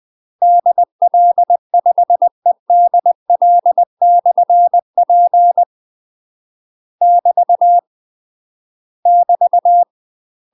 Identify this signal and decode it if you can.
Morse code, heard as a keyed tone, about 20 wpm, 700 Hz. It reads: DL5EDL/P <BT> <BT>
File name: UK MASTER QSOs TRAINING/20WPM